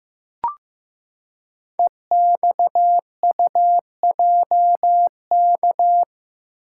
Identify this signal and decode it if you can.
Morse code, heard as a keyed tone, about 15 wpm, 700 Hz. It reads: EXUJK
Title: Code Groups - 5 Letters - 15wpm